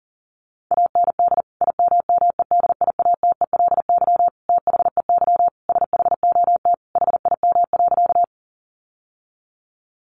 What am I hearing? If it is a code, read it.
Morse code, heard as a keyed tone, about 40 wpm, 700 Hz. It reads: AND IMMEDIATELY THEY SHOT HIM.
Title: Sentences from Top 1100 Words 40wpm